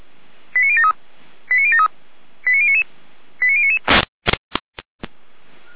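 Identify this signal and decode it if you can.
Five-tone selective call (ZVEI), heard as quick runs of five tones, tones of about 70 ms. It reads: 89073 89073 89099 89099
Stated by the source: ZVEI five-tone selective call received on SDR, logged by feuerwehr-orlamuende.org (Feuerwehr Orlamünde fire brigade).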